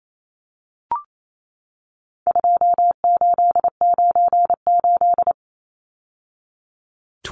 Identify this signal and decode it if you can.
Morse code, heard as a keyed tone, about 28 wpm, 700 Hz. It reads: 2898
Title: Numbers - 4-Digit 28wpm